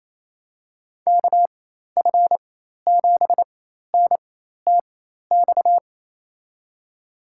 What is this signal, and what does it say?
Morse code, heard as a keyed tone, about 28 wpm, spaced as for 13 wpm, 700 Hz. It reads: KF7NTX
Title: USA Call Signs - 6 Characters - 4x Character Spacing - 28wpm